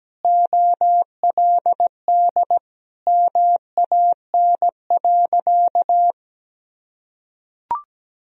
Morse code, heard as a keyed tone, about 17 wpm, 700 Hz. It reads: OLD MAN.
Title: Sentences from Top 1000 Words - Review Part 2 - 17wpm